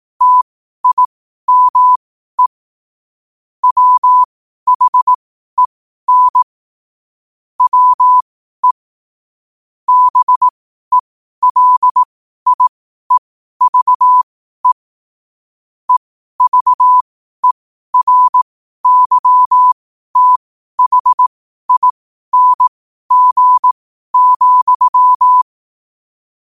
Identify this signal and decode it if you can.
Morse code, heard as a keyed tone, about 18 wpm, spaced as for 12 wpm, 1000 Hz. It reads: TIME WHEN WE BELIEVE EVERYTHING,
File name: qotdfiles / 12wpm